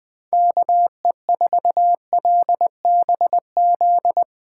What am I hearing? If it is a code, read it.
Morse code, heard as a keyed tone, about 20 wpm, 700 Hz. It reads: KE4LBZ
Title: USA Call Signs - 6 Characters - 3x Character Spacing - 20wpm